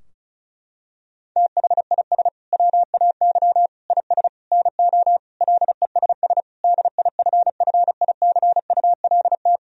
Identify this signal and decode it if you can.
Morse code, heard as a keyed tone, about 34 wpm, 700 Hz. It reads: THIS WAY IS NO LESS DIFFICULT
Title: Sentences from Top 1000 Words - Review Part 2 - 35wpm